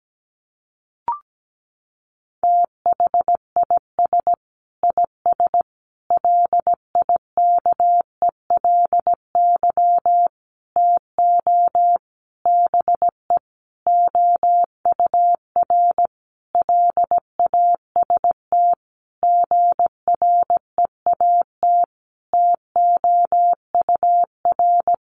Morse code, heard as a keyed tone, about 17 wpm, 700 Hz. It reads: THIS IS LIKELY TO BE OUR LAST GREAT TOUR
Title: Sentences from Top 1400 Words 17wpm